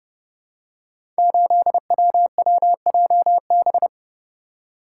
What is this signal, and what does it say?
Morse code, heard as a keyed tone, about 30 wpm, 700 Hz. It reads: 8WWJB
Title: Code Groups - 5 Letter:Numbers - 30wpm